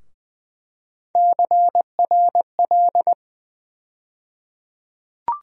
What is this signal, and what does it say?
Morse code, heard as a keyed tone, about 20 wpm, 700 Hz. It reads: CRL
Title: USA Call Sign Postfixes - 3 Letters 20wpm